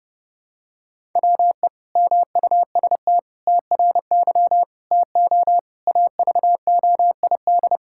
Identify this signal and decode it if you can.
Morse code, heard as a keyed tone, about 30 wpm, 700 Hz. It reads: WE MUST TRY TO AVOID
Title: Sentences from Top 1000 Words - Review Part 6 - 30wpm